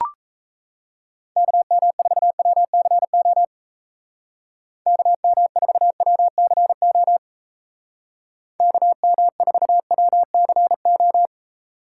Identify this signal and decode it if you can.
Morse code, heard as a keyed tone, about 34 wpm, 700 Hz. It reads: KM4WCO KM4WCO KM4WCO
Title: Speed Racing - USA 2x3 Call Signs 28wpm